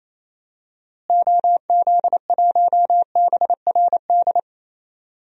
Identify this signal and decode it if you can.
Morse code, heard as a keyed tone, about 28 wpm, 700 Hz. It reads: OZ1BRD
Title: DXCC-100 Call Signs - 6 Characters - Repeated 3x - 28wpm